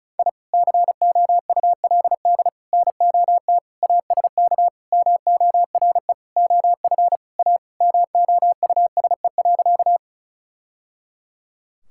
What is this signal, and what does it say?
Morse code, heard as a keyed tone, about 36 wpm, 700 Hz. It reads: I COULD NOT ASK MORE OF A MOUSE.
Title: Sentences from Top 1000 Words - Review Part 5 - 35wpm